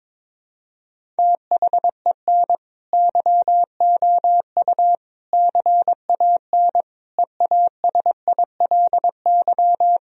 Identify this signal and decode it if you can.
Morse code, heard as a keyed tone, about 22 wpm, 700 Hz. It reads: THEN YOU CAN EASILY
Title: Sentences from Top 1000 Words - Review Part 1 - 22wpm